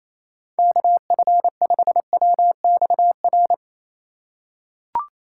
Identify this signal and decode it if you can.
Morse code, heard as a keyed tone, about 28 wpm, 700 Hz. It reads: KF5WXR
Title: USA Call Signs - 3x Character Spacing - 28wpm